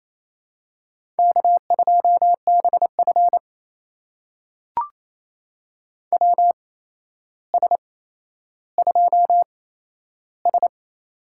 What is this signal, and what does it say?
Morse code, heard as a keyed tone, about 28 wpm, spaced as for 12 wpm, 700 Hz. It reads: K2BF WS2S
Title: USA Call Signs - 4 Characters - 8x Character Spacing - 28wpm